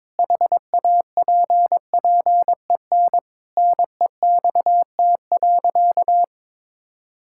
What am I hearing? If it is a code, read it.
Morse code, heard as a keyed tone, about 21 wpm, 700 Hz. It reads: HAPPEN NEXT.
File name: Sentences with 7-Word Phrases - 22wpm